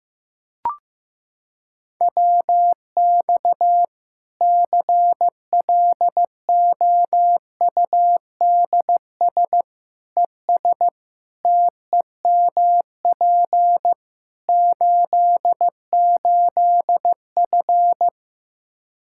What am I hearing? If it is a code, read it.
Morse code, heard as a keyed tone, about 15 wpm, 700 Hz. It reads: WX CLOUDS ES TEMP 88F
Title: K1USN SST QSO Practice 15wpm